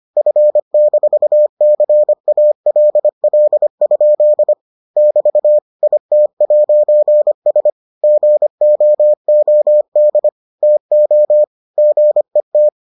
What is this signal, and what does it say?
Morse code, heard as a keyed tone, about 25 wpm, 600 Hz. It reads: F-CALL? <BT> IT'S GOOD TO GET